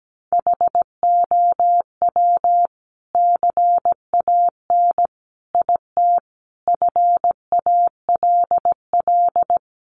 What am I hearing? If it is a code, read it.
Morse code, heard as a keyed tone, about 17 wpm, 700 Hz. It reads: HOW CAN IT FALL
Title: Sentences from Top 1100 Words 17wpm